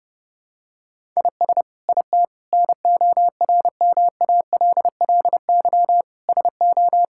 Morse code, heard as a keyed tone, about 30 wpm, 700 Hz. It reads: IS IT NORMALLY SO
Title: Sentences from Top 1000 Words - Review Part 6 - 30wpm